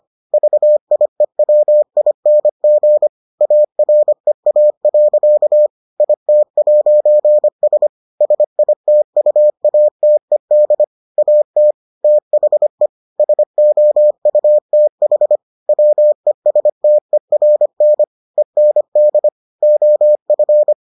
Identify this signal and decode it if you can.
Morse code, heard as a keyed tone, about 25 wpm, 600 Hz. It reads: VIEWING AREA. IT'S SITUATED AT THE SOUTH WESTERN END OF